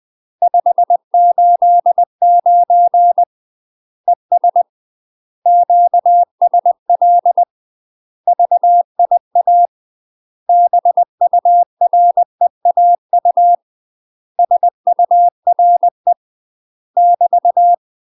Morse code, heard as a keyed tone, about 20 wpm, 700 Hz. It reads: 589 ES QSL VIA BUREAU SURE <BT>
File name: UK MASTER QSOs TRAINING / 20WPM / UK MASTER QSOs 0029